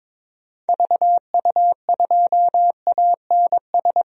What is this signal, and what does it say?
Morse code, heard as a keyed tone, about 22 wpm, 700 Hz. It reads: VU2ANS
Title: DXCC-100 Call Signs - 6 Characters - Repeated 3x - 22wpm